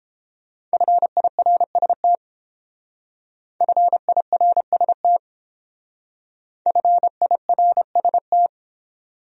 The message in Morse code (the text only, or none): FIRST FIRST FIRST